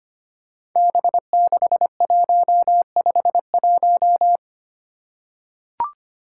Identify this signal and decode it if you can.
Morse code, heard as a keyed tone, about 25 wpm, 700 Hz. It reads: B6151
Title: Five Challenging Letter-Number Combinations 25wpm